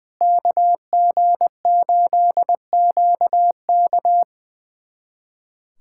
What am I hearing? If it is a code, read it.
Morse code, heard as a keyed tone, about 21 wpm, 700 Hz. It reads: KG8QK
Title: USA Callsign DE Callsign - 5 Characters - 20wpm